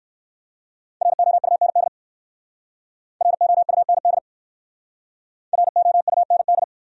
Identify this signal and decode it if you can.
Morse code, heard as a keyed tone, about 55 wpm, 700 Hz. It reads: ROUND ROUND ROUND
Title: Speed-Racing Top 200 Words 45wpm